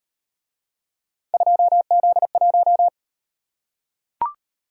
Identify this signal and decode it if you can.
Morse code, heard as a keyed tone, about 37 wpm, 700 Hz. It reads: 2Z1